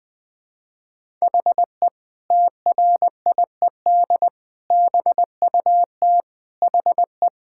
Morse code, heard as a keyed tone, about 20 wpm, 700 Hz. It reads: HE TRIED BUT HE
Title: Sentences with 2-Word Phrases - Spoken Beginning - 20wpm